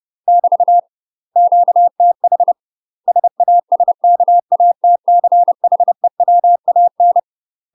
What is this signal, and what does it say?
Morse code, heard as a keyed tone, about 30 wpm, 700 Hz. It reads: <BT> QTH SASKATCHEWAN
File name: UK MASTER QSOs TRAINING / 30WPM / UK MASTER QSOs 0035